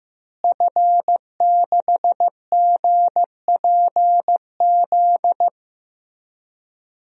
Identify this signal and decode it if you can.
Morse code, heard as a keyed tone, about 15 wpm, 700 Hz. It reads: F6GPZ
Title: DXCC-100 Call Signs - 5 Characters - Repeated 3x - 15wpm